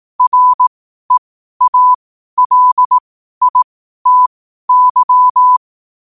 Morse code, heard as a keyed tone, about 18 wpm, spaced as for 12 wpm, 1000 Hz. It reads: REALITY
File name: qotdfiles / 12wpm